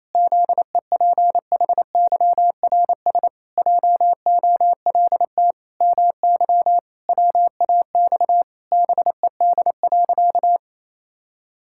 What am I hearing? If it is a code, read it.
Morse code, heard as a keyed tone, about 28 wpm, 700 Hz. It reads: ZEPHYRS JOLT MY WAX BED.
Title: Pangram Challenge v2 - 28wpm